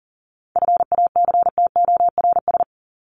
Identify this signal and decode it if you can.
Morse code, heard as a keyed tone, about 40 wpm, 700 Hz. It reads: FACTORS
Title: ES:S Suffix - Top 100 Words with Base Words 40wpm